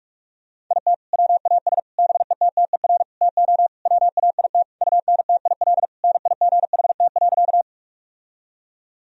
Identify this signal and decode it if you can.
Morse code, heard as a keyed tone, about 41 wpm, 700 Hz. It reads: IT WAS BETTER TO WAIT UNTIL NIGHT.
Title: Sentences from Top 1000 Words - Review Part 1 - 45wpm